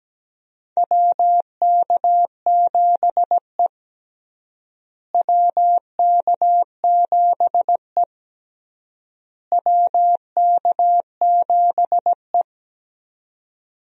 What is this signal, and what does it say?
Morse code, heard as a keyed tone, about 17 wpm, 700 Hz. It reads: WK7E WK7E WK7E
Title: USA Call Signs - 4 Characters - Repeated 3x - 17wpm